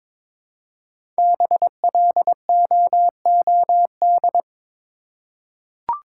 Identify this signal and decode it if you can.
Morse code, heard as a keyed tone, about 22 wpm, 700 Hz. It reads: BLOOD
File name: 5 Letter Words - First 3 Letters Given - 22wpm